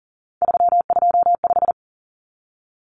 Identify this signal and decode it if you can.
Morse code, heard as a keyed tone, about 40 wpm, 700 Hz. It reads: 325
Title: Numbers - 3-Digit 40wpm